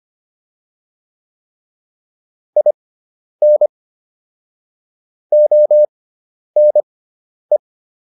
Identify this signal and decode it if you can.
Morse code, heard as a keyed tone, about 25 wpm, spaced as for 10 wpm, 600 Hz. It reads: IN ONE